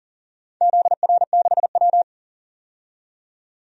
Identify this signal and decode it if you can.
Morse code, heard as a keyed tone, about 40 wpm, 700 Hz. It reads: ZR6W
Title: DXCC-100 Call Signs - 4 Characters - 40wpm